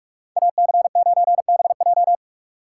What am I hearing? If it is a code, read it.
Morse code, heard as a keyed tone, about 46 wpm, 700 Hz. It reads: AC9BJ
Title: USA Call Signs - 5 Characters - 2x Character Spacing - 45wpm